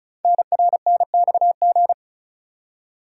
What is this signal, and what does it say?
Morse code, heard as a keyed tone, about 36 wpm, 690 Hz. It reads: NRNXG
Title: Code Groups - 5 Letters - 35wpm